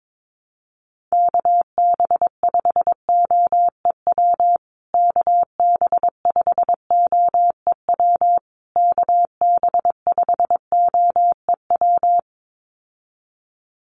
Morse code, heard as a keyed tone, about 22 wpm, 700 Hz. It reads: KB5OEW KB5OEW KB5OEW
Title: USA Call Signs Repeated 3x 22wpm